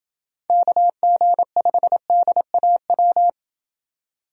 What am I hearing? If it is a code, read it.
Morse code, heard as a keyed tone, about 27 wpm, 700 Hz. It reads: KG5DAW